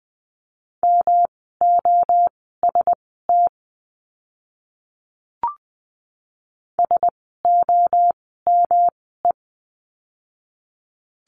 Morse code, heard as a keyed tone, about 20 wpm, spaced as for 15 wpm, 700 Hz. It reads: MOST SOME
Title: 237 Lesson 38f - Letter q - Words 20wpm